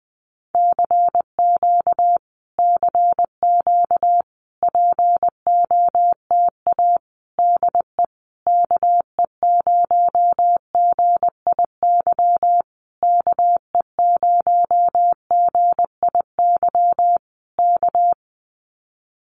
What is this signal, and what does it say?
Morse code, heard as a keyed tone, about 20 wpm, 700 Hz. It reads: CQ CQ POTA DE KE0GIY KE0GIY K
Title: Common QSO Phrases Encore 20wpm